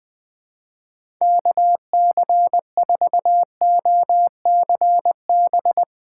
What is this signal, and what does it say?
Morse code, heard as a keyed tone, about 20 wpm, 700 Hz. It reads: KC4OCB
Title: USA Call Signs - 2x Character Spacing - 20wpm